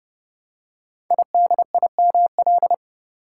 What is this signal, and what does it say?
Morse code, heard as a keyed tone, about 30 wpm, 700 Hz. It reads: IDIML
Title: Code Groups - 5 Letter:Numbers - 30wpm